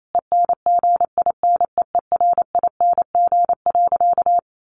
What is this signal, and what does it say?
Morse code, heard as a keyed tone, about 28 wpm, 700 Hz. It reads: ENGINEERING.
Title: Sentences from Top 1500 Words 28wpm